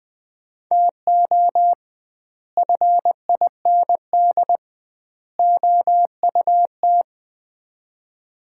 Morse code, heard as a keyed tone, about 20 wpm, 700 Hz. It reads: TO FIND OUT
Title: 2x Word-Spacing - Sets of 3 Words - 20wpm